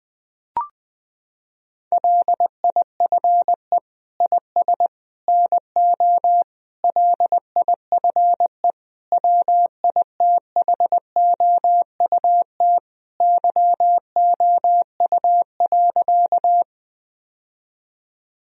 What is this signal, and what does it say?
Morse code, heard as a keyed tone, about 20 wpm, 700 Hz. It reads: LIFE IS NO LIFE WITHOUT YOU.
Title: Sentences from Top 1000 Words - Review Part 3 - 20wpm